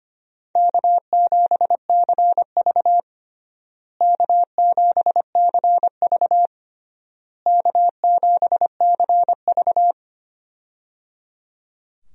Morse code, heard as a keyed tone, about 25 wpm, 700 Hz. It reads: K7CV K7CV K7CV
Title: USA Call Signs - 4 Characters - Repeated 3x - 25wpm